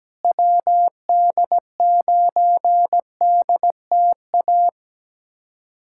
Morse code, heard as a keyed tone, about 17 wpm, 700 Hz. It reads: WD9DTA
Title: USA Callsign DE Callsign - 6 Characters - 17wpm